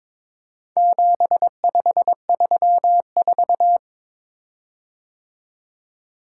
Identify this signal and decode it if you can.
Morse code, heard as a keyed tone, about 22 wpm, 700 Hz. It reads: 7534
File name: practice / 22wpm / Four Letter-Number Combinations - 22wpm